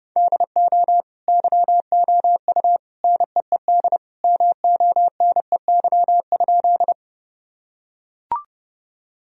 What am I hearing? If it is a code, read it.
Morse code, heard as a keyed tone, about 32 wpm, 700 Hz. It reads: DO YOU NEED MONEY?
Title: Sentences from Top 1000 Words - Review Part 1 - 30wpm